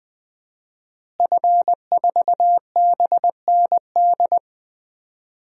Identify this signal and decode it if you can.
Morse code, heard as a keyed tone, about 20 wpm, 700 Hz. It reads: F4BND
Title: DXCC-100 Call Signs - 5 Characters - Repeated 3x - 20wpm